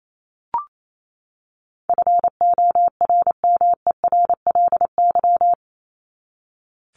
Morse code, heard as a keyed tone, about 28 wpm, 700 Hz. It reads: FORMERLY
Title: LY Suffix - Top 100 Words with Base Words 28wpm